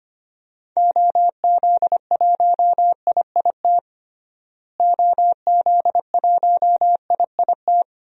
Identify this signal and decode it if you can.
Morse code, heard as a keyed tone, about 25 wpm, 700 Hz. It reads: OZ1IIT OZ1IIT
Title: DXCC-100 Call Signs - 6 Characters - Repeated 3x - 25wpm